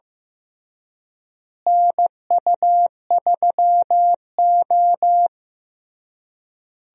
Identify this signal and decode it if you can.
Morse code, heard as a keyed tone, about 15 wpm, 700 Hz. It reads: NU3O
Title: USA Call Signs - 4 Characters - Repeated 3x - 15wpm